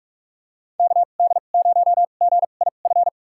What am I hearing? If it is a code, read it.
Morse code, heard as a keyed tone, about 41 wpm, spaced as for 31 wpm, 700 Hz. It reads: KD0GIF
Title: USA Call Signs - 6 Characters - 2x Character Spacing - 45wpm